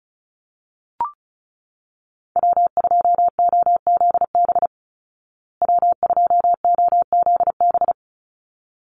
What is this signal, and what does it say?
Morse code, heard as a keyed tone, about 34 wpm, 700 Hz. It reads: W2OZB W2OZB
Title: Call Signs Repeated 3x - 35wpm